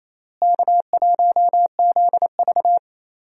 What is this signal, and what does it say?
Morse code, heard as a keyed tone, about 28 wpm, 700 Hz. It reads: K1ZV
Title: USA Call Signs - 4 Characters - Repeated 2x - 28wpm